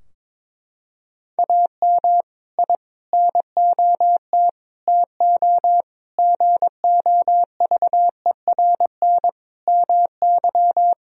Morse code, heard as a keyed tone, about 22 wpm, 700 Hz. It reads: AM I NOT TO GOVERN MY
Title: Sentences from Top 1000 Words - Review Part 2 - 22wpm